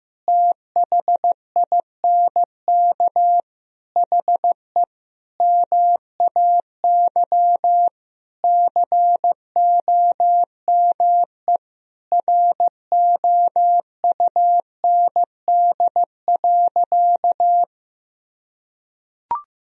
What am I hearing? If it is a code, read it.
Morse code, heard as a keyed tone, about 15 wpm, 700 Hz. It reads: THINK HE MAY COME ROUND.
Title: Sentences from Top 1000 Words - Review Part 1 - 15wpm